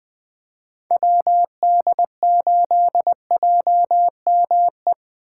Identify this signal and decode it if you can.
Morse code, heard as a keyed tone, about 24 wpm, 700 Hz. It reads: WD8JME
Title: Speed Racing - USA 2x3 Call Signs 20wpm